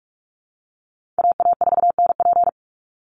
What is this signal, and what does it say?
Morse code, heard as a keyed tone, about 44 wpm, 700 Hz. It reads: AA4NP
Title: Call Signs Repeated 2x - 45wpm